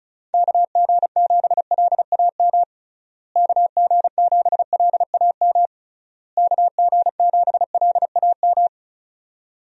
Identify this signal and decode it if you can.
Morse code, heard as a keyed tone, about 36 wpm, 700 Hz. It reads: KG7LAM KG7LAM KG7LAM
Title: USA Call Signs - 6 Characters - Repeated 3x - 35wpm